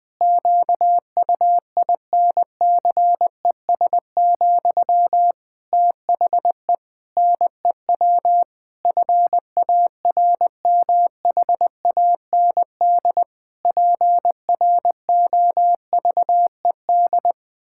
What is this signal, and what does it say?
Morse code, heard as a keyed tone, about 20 wpm, 700 Hz. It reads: QUINCES, THE NEW FARMHAND PROVED
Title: Pangram Challenge v2 - 20wpm